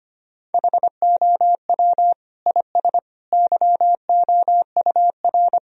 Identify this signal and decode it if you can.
Morse code, heard as a keyed tone, about 25 wpm, 700 Hz. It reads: HOW IS YOUR
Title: Sentences from Top 1000 Words - Review Part 3 - 25wpm